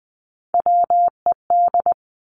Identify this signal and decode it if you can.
Morse code, heard as a keyed tone, about 20 wpm, 700 Hz. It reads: WED
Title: Three Challenging Letter-Number Combinations 20wpm